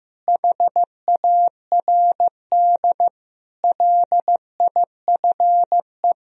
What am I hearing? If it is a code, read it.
Morse code, heard as a keyed tone, about 16 wpm, 700 Hz. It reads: HARD LIFE
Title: Sentences from Top 1000 Words - Review Part 2 - 15wpm